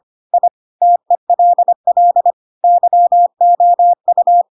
Morse code, heard as a keyed tone, about 25 wpm, 700 Hz. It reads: I TELL YOU
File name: Pinocchio / Pinocchio ch4 25 25 ik1hge